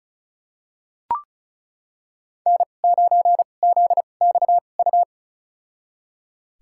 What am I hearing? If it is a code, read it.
Morse code, heard as a keyed tone, about 36 wpm, spaced as for 25 wpm, 700 Hz. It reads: N9ZXU
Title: USA Call Signs - 5 Characters - 2x Character Spacing - 35wpm